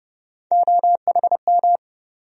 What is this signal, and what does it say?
Morse code, heard as a keyed tone, about 30 wpm, 700 Hz. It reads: OHM
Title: USA Call Sign Postfixes - 3 Letters 30wpm